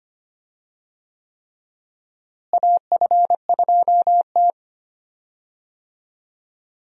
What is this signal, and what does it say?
Morse code, heard as a keyed tone, about 25 wpm, 700 Hz. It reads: AF2T
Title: USA Call Signs - 4 Characters - Repeated 3x - 25wpm